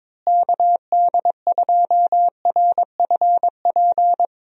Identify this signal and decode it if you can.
Morse code, heard as a keyed tone, about 22 wpm, 700 Hz. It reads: KD2RFP
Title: USA Call Signs - 6 Characters - Repeated 3x - 22wpm